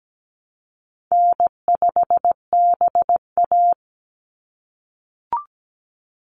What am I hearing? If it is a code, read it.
Morse code, heard as a keyed tone, about 17 wpm, 700 Hz. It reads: N5BA
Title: Easy Call Signs - 17wpm